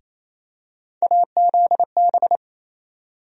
Morse code, heard as a keyed tone, about 28 wpm, 700 Hz. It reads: AZB
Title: USA Call Sign Postfixes - 3 Letters 28wpm